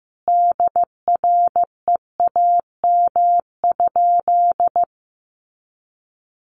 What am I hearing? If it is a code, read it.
Morse code, heard as a keyed tone, about 15 wpm, 700 Hz. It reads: DREAM?
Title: Sentences from Top 1500 Words 15wpm